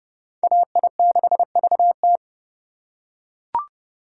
Morse code, heard as a keyed tone, about 28 wpm, 700 Hz. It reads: AI6VT